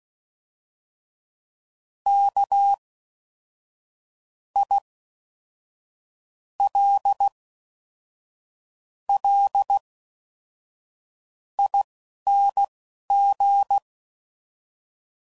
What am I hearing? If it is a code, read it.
Morse code, heard as a keyed tone, about 16 wpm, spaced as for 11 wpm, 800 Hz. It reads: K I L L ING
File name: Suffixes / Special INGs / Fast INGs